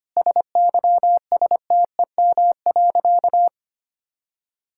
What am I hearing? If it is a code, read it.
Morse code, heard as a keyed tone, about 25 wpm, 700 Hz. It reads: SYSTEM.
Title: Sentences from Top 1000 Words - Review Part 5 - 25wpm